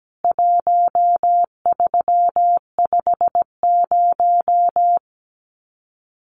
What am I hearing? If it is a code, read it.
Morse code, heard as a keyed tone, about 17 wpm, 700 Hz. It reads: 1350
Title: CWT Exchange 17wpm